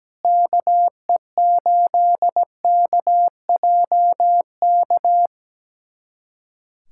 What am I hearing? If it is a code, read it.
Morse code, heard as a keyed tone, about 17 wpm, 700 Hz. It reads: KE8KJK
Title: USA Callsign DE Callsign 17wpm